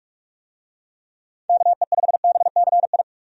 Morse code, heard as a keyed tone, about 46 wpm, 700 Hz. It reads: KE5BCI
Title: USA Call Signs - 6 Characters - 45wpm